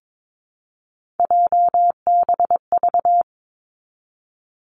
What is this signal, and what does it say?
Morse code, heard as a keyed tone, about 22 wpm, 700 Hz. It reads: JBV
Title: Three Challenging Letter-Number Combinations 22wpm